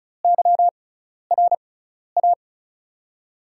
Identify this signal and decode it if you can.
Morse code, heard as a keyed tone, about 36 wpm, spaced as for 12 wpm, 700 Hz. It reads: YRA
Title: USA Call Signs - 6 Characters - 6x Character Spacing - 35wpm